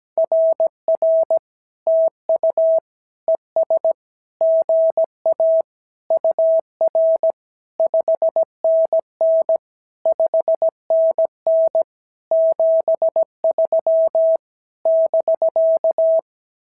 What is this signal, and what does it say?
Morse code, heard as a keyed tone, about 17 wpm, 690 Hz. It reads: RR TU ES GA UR 5NN 5NN 73 <BK>
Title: SOTA QSO Practice 17wpm